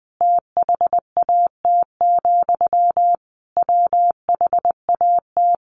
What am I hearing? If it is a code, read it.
Morse code, heard as a keyed tone, about 20 wpm, 700 Hz. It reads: THAT, WHAT
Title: Sentences from Top 600 Words 20wpm